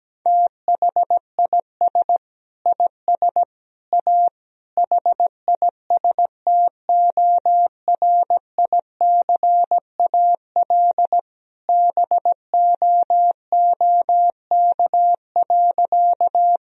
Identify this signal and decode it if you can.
Morse code, heard as a keyed tone, about 17 wpm, 700 Hz. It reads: THIS IS A HISTORICAL BOOK.
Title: Sentences from Top 1000 Words - Review Part 4 - 17wpm